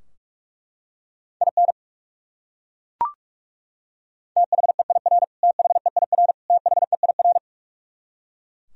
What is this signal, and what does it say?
Morse code, heard as a keyed tone, about 41 wpm, 700 Hz. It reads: IN THEIR THEIR THEIR
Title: Top 100 Words Repeated 45wpm